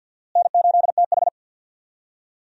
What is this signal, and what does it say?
Morse code, heard as a keyed tone, about 46 wpm, 700 Hz. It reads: N8TH
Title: USA Call Signs - 4 Characters - Repeated 3x - 50wpm